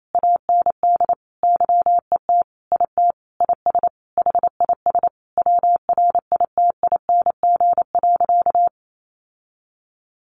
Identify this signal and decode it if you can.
Morse code, heard as a keyed tone, about 28 wpm, 700 Hz. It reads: AND YET IT IS HIS WRITING.
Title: Sentences from Top 800 Words 28wpm